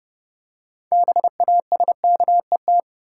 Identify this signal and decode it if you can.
Morse code, heard as a keyed tone, about 30 wpm, 700 Hz. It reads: BASKET
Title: Six Letter Words - 30wpm